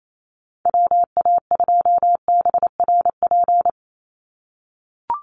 Morse code, heard as a keyed tone, about 28 wpm, 700 Hz. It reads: WA2BRP
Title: USA Call Signs 28wpm